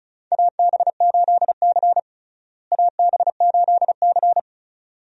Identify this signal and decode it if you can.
Morse code, heard as a keyed tone, about 36 wpm, 700 Hz. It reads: AB8C AB8C
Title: USA Call Signs - 4 Characters - Repeated 2x - 35wpm